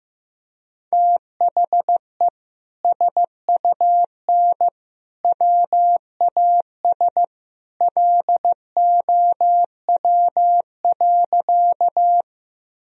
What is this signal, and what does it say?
Morse code, heard as a keyed tone, about 15 wpm, 700 Hz. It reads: THE SUN WAS LOW.
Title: Sentences from Top 1000 Words - Review Part 1 - 15wpm